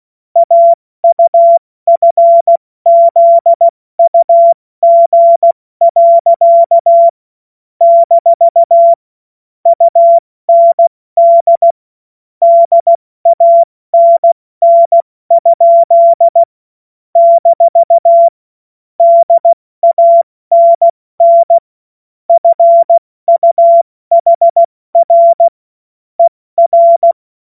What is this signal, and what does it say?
Morse code, heard as a keyed tone, about 16 wpm, 680 Hz. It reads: AUFZUG. - UND DANN? - DANN FUHR ER
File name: Morsekurs / Witze / Witz 12 März 2024 70 F80